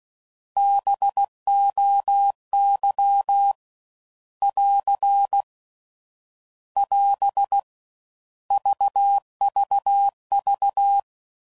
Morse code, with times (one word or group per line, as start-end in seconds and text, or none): 0.6-3.5 BOY
4.4-5.4 <AR>
6.8-7.6 <AS>
8.5-11.0 VVV